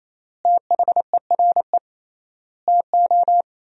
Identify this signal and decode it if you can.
Morse code, heard as a keyed tone, about 28 wpm, 700 Hz. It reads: THERE TO
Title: 3x Word-Spacing - Sentences from Top 100 Words - 28wpm